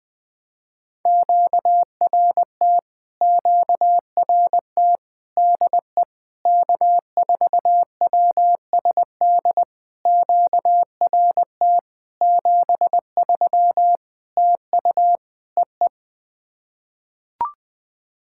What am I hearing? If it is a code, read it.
Morse code, heard as a keyed tone, about 20 wpm, 700 Hz. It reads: QRT QRT DE K4WSD QRT 73 TU EE
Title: POTA QSO Practice - 20wpm